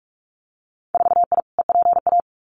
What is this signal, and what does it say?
Morse code, heard as a keyed tone, about 44 wpm, 700 Hz. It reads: 4I EPA
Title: Winter Field Day Exchanges 45wpm